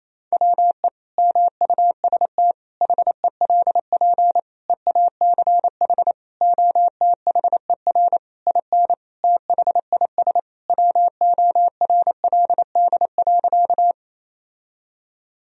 Morse code, heard as a keyed tone, about 28 wpm, 700 Hz. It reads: WE MUST HELP EACH OTHER IN THIS WORLD.
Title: Sentences from Top 1000 Words - Review Part 1 - 28wpm